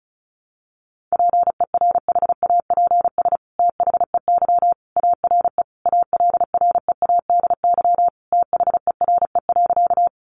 Morse code, heard as a keyed tone, about 34 wpm, 700 Hz. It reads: PERHAPS THEY ARE ALREADY THERE.
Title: Sentences from Top 700 Words 35wpm